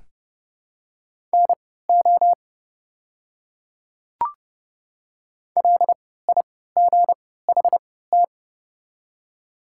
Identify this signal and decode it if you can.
Morse code, heard as a keyed tone, about 30 wpm, spaced as for 17 wpm, 700 Hz. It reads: NO LIGHT
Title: 3x Character Spacing - Top 200 Words - 30wpm